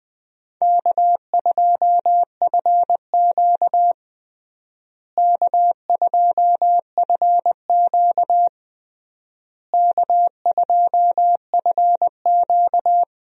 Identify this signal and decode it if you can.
Morse code, heard as a keyed tone, about 21 wpm, 700 Hz. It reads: K2FQ K2FQ K2FQ
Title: USA Call Signs - 4 Characters - Repeated 3x - 20wpm